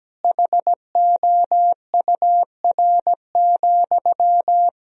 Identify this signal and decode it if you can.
Morse code, heard as a keyed tone, about 17 wpm, 700 Hz. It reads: HOUR,
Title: Sentences from Top 1000 Words - Review Part 2 - 17wpm